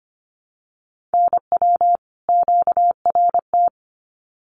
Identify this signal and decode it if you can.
Morse code, heard as a keyed tone, about 25 wpm, 700 Hz. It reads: NW QRT
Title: Common QSO Phrases 25wpm